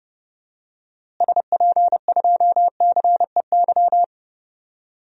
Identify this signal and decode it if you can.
Morse code, heard as a keyed tone, about 30 wpm, 700 Hz. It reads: SP2CEY
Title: DXCC-100 Call Signs - 6 Characters - Repeated 3x - 30wpm